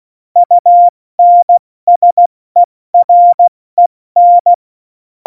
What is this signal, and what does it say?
Morse code, heard as a keyed tone, about 16 wpm, 700 Hz. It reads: UNSEREN
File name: Morsekurs / Witze / Witz 24. November 2020, 70 F80